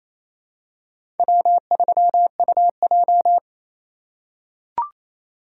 Morse code, heard as a keyed tone, about 28 wpm, 700 Hz. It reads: W3UJ
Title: USA Call Signs - 4 Characters - 2x Character Spacing - 28wpm